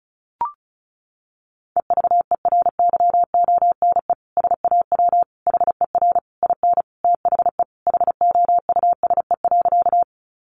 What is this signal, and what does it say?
Morse code, heard as a keyed tone, about 34 wpm, 700 Hz. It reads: EVERYONE SAW HER IN THE HOUSE.
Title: Sentences from Top 800 Words 35wpm